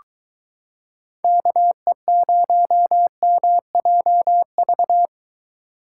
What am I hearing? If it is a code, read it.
Morse code, heard as a keyed tone, about 22 wpm, 700 Hz. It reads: KE0MJV